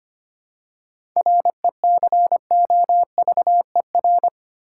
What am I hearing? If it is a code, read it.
Morse code, heard as a keyed tone, about 25 wpm, 700 Hz. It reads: RECOVER
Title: Seven Letter Words - 25wpm